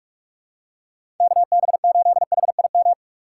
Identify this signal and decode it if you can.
Morse code, heard as a keyed tone, about 46 wpm, 700 Hz. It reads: KB8HIM
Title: USA Call Signs - 2x Character Spacing - 45wpm